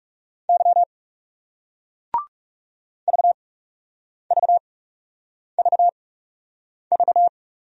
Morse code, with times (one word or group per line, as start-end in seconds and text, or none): 0.5-0.8 Y
3.1-7.3 VVVV